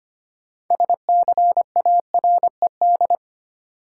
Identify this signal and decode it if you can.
Morse code, heard as a keyed tone, about 25 wpm, 700 Hz. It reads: SCARED
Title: Sentences from Top 700 Words - New Word Drill - 25wpm